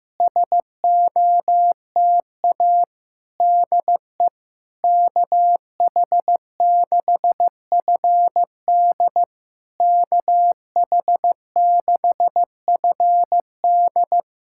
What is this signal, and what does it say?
Morse code, heard as a keyed tone, about 15 wpm, 690 Hz. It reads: SOTA DE KH6FD KH6FD
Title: SOTA QSO Practice 15wpm